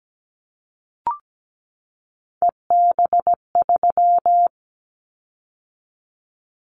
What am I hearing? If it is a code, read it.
Morse code, heard as a keyed tone, about 17 wpm, 700 Hz. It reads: EB3
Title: Callsign Prefixes for Top 25 DXCC - 17wpm